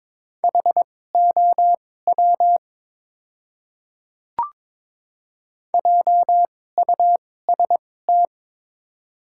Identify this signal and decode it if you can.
Morse code, heard as a keyed tone, about 22 wpm, spaced as for 16 wpm, 700 Hz. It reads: HOW JUST
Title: 2x Character Spacing - Top 300 Words - 22wpm